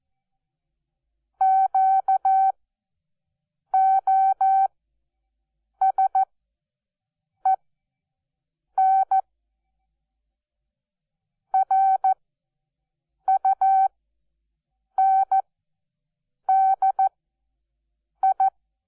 Morse code, heard as a keyed tone, about 14 wpm, spaced as for 6 wpm, 770 Hz. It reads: QOSEN RUNDI
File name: Onlinelehrgang / morsen / DJ4UF Morsen12